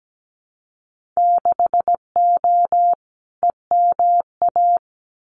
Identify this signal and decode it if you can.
Morse code, heard as a keyed tone, about 17 wpm, 700 Hz. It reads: 6O EMA
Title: Winter Field Day Exchanges Encore 17wpm